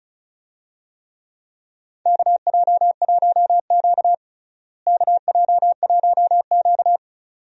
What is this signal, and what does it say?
Morse code, heard as a keyed tone, about 36 wpm, 700 Hz. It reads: KJ1Q KJ1Q
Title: USA Call Signs - 4 Characters - Repeated 2x - 35wpm